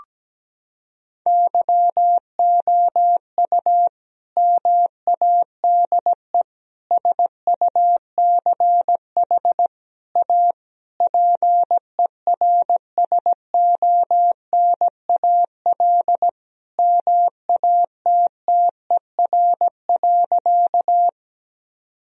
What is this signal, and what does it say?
Morse code, heard as a keyed tone, about 17 wpm, 700 Hz. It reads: YOU MADE SUCH A PERSONAL MATTER.
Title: Sentences from Top 1000 Words - Review Part 4 - 17wpm